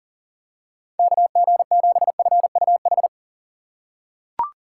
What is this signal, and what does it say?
Morse code, heard as a keyed tone, about 39 wpm, 700 Hz. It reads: KG7FUH
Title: USA Call Signs - 6 Characters - 3x Character Spacing - 40wpm